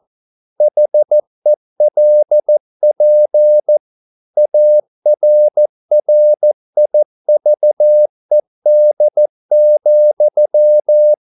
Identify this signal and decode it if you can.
Morse code, heard as a keyed tone, about 14 wpm, 600 Hz. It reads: HELP ARRIVED,